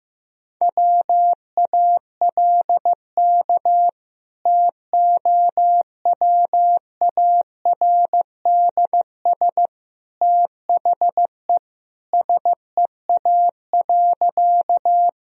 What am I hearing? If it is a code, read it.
Morse code, heard as a keyed tone, about 16 wpm, 700 Hz. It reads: WALK TOWARDS THE SEA.
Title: Sentences from Top 1000 Words - Review Part 2 - 15wpm